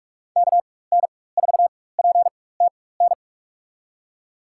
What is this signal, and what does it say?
Morse code, heard as a keyed tone, about 41 wpm, spaced as for 20 wpm, 700 Hz. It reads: KN4PTN
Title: USA Call Signs - 6 Characters - 4x Character Spacing - 45wpm